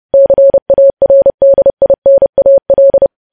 Morse code, heard as a keyed tone, about 30 wpm, 550 Hz. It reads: CARDINAL
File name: podcast / News headlines-1x-30x30